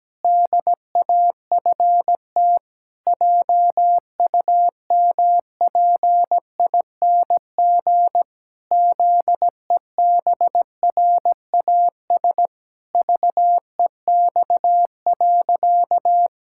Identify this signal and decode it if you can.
Morse code, heard as a keyed tone, about 17 wpm, 700 Hz. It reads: DAFT JUMPING ZEBRAS VEX.
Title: Pangram Challenge v2 - 17wpm